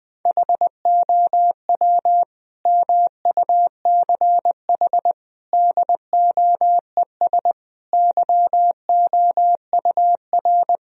Morse code, heard as a keyed tone, about 20 wpm, 700 Hz. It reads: HOW MUCH DOES YOUR
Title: Sentences from Top 1000 Words - Review Part 4 - 20wpm